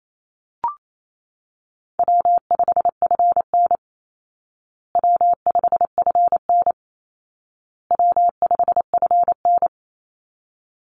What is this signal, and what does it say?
Morse code, heard as a keyed tone, about 28 wpm, 700 Hz. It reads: W5FN W5FN W5FN
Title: Easy Call Signs Repeated 3x - 28wpm